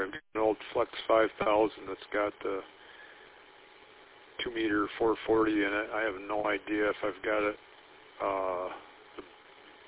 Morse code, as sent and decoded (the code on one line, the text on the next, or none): none